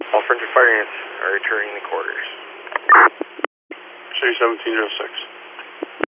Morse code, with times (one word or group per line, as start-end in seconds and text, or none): none